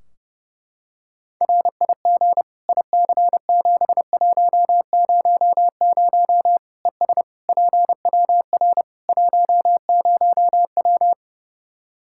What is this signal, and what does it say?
Morse code, heard as a keyed tone, about 32 wpm, 700 Hz. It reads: RIG IC7100 ES PWR 10W
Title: K1USN SST QSO Practice 30wpm